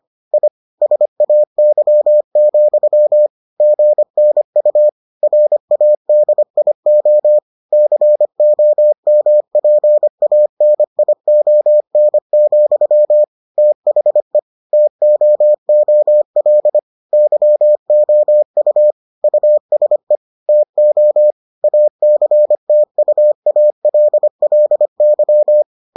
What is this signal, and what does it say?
Morse code, heard as a keyed tone, about 25 wpm, 600 Hz. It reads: I SAY, GNU RADIO COMPANION, THE TOOL YOU USE TO ACTUALLY